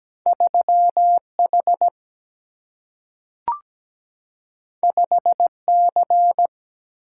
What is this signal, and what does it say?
Morse code, heard as a keyed tone, about 17 wpm, 700 Hz. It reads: 3H 5C
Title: Single-Letter Single-Number 17wpm